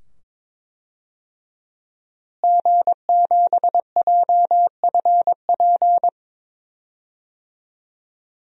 Morse code, heard as a keyed tone, about 23 wpm, 700 Hz. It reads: G7JFP
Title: DXCC-100 Call Signs - 5 Characters - Repeated 2x - 22wpm